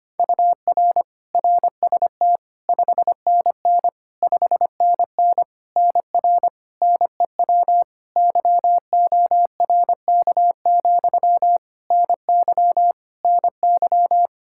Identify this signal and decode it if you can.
Morse code, heard as a keyed tone, about 25 wpm, 700 Hz. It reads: UR RST 5NN 5NN NR NEW YORK, NY NY
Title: K1USN SST QSO Practice 25wpm